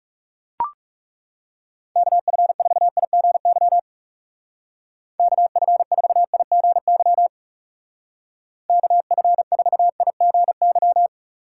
Morse code, heard as a keyed tone, about 36 wpm, 700 Hz. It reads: KF4IGY KF4IGY KF4IGY
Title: Speed Racing - USA 2x3 Call Signs 30wpm